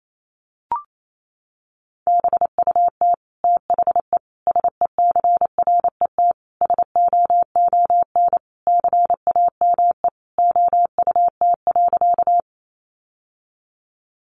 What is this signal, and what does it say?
Morse code, heard as a keyed tone, about 28 wpm, 700 Hz. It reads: BUT THE SECRET SOON CAME OUT.
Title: Sentences from Top 1200 Words 28wpm